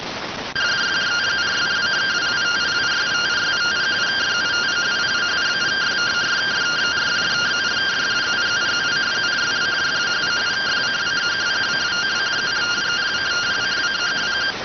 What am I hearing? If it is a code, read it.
Two-tone teleprinter signal, about 170 Hz shift.